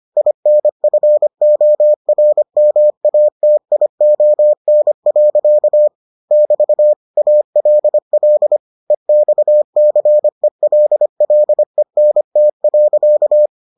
Morse code, heard as a keyed tone, about 25 wpm, 600 Hz. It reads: INFORMATION. <BT> ALL EXCELLENT.